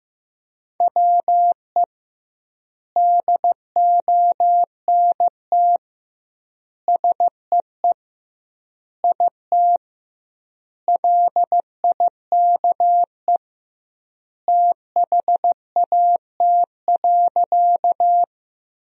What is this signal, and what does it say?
Morse code, heard as a keyed tone, about 15 wpm, 700 Hz. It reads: WE DONT SEE IT LIKE THAT.
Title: 2x Word-Spacing - Sentences from Top 300 Words - 15wpm